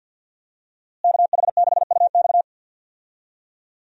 Morse code, heard as a keyed tone, about 52 wpm, 700 Hz. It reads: KH6UX
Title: USA Call Signs - 5 Characters - Repeated 3x - 50wpm